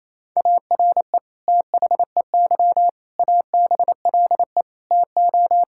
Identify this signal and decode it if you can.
Morse code, heard as a keyed tone, about 28 wpm, 700 Hz. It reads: ARE THEY ABLE TO
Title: Sentences from Top 1000 Words - Review Part 5 - 28wpm